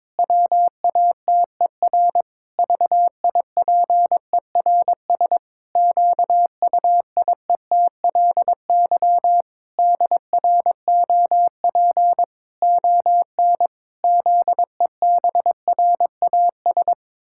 Morse code, heard as a keyed tone, about 23 wpm, 700 Hz. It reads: WATER VIPERS QUIETLY DROP ON ZEBRAS
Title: Pangram Challenge v2 - 22wpm